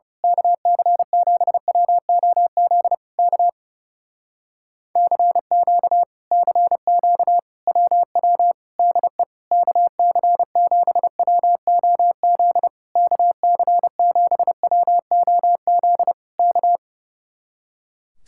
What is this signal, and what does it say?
Morse code, heard as a keyed tone, about 28 wpm, 700 Hz. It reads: KC7WOZ K CQ CQ WW DE KC7WOZ KC7WOZ K
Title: Speed-Racing Common QSO Phrases - 30wpm